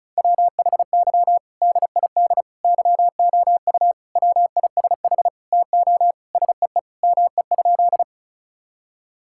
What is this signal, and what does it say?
Morse code, heard as a keyed tone, about 34 wpm, 700 Hz. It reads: WHY DID YOU WISH TO SEE ME?
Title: Sentences from Top 1600 Words 35wpm